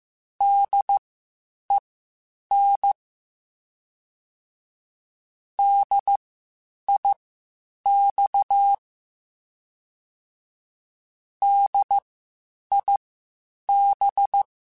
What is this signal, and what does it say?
Morse code, heard as a keyed tone, about 15 wpm, spaced as for 8 wpm, 800 Hz. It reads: DEN DIX DIB
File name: Spaced Alphas / Spaced Alphas mp3s / Ds sp @13